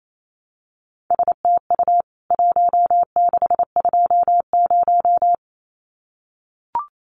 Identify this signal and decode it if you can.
Morse code, heard as a keyed tone, about 28 wpm, 700 Hz. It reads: STU 1620
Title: CWT Exchange 28wpm